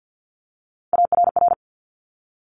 Morse code, heard as a keyed tone, about 50 wpm, 700 Hz. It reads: ARR
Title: Top Three Letter Combinations in Words 50wpm